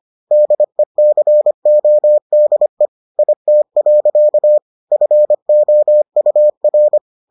Morse code, heard as a keyed tone, about 25 wpm, 600 Hz. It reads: DECODE IT. FOUR